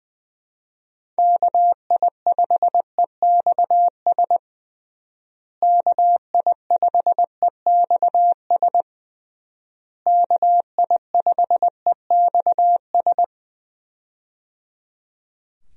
Morse code, heard as a keyed tone, about 21 wpm, 700 Hz. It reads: KI5EXS KI5EXS KI5EXS
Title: USA Call Signs - 6 Characters - Repeated 3x - 20wpm